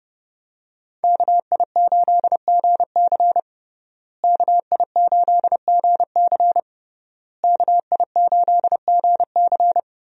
Morse code, heard as a keyed tone, about 32 wpm, 700 Hz. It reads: KI8GC KI8GC KI8GC
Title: USA Call Signs - 5 Characters - Repeated 3x - 30wpm